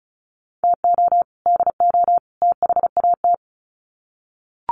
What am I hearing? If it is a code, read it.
Morse code, heard as a keyed tone, about 34 wpm, 700 Hz. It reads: TO DO THAT
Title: Sets of 3 Words 35wpm